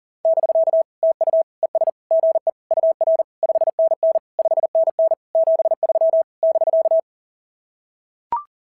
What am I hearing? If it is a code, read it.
Morse code, heard as a keyed tone, about 40 wpm, 690 Hz. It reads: <BK> TU ES GE UR 5NN 5NN 73 <BK>
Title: SOTA QSO Practice 40wpm